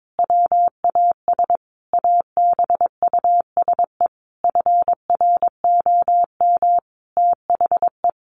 Morse code, heard as a keyed tone, about 22 wpm, 700 Hz. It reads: WAS ABUSE FROM THE
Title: Sentences from Top 1300 Words 22wpm